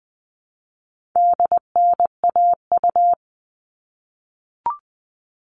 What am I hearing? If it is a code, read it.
Morse code, heard as a keyed tone, about 20 wpm, 700 Hz. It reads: DNAU
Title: All Cut Numbers - 4-Digit 20wpm